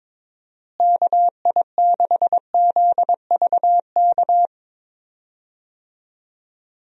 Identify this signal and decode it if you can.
Morse code, heard as a keyed tone, about 23 wpm, 700 Hz. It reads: KI6ZVK
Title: USA Call Signs - 6 Characters - Repeated 3x - 22wpm